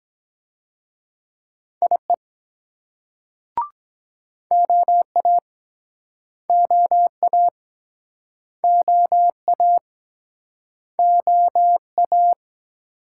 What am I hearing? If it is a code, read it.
Morse code, heard as a keyed tone, about 23 wpm, 700 Hz. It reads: IE OA OA OA OA